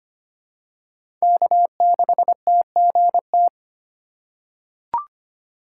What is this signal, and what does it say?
Morse code, heard as a keyed tone, about 25 wpm, 700 Hz. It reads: K6TGT
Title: USA Call Signs - 5 Characters - 4x Character Spacing - 25wpm